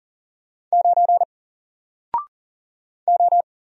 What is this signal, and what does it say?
Morse code, heard as a keyed tone, about 40 wpm, 700 Hz. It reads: 9 O
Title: Single Letter-Number Alt 40wpm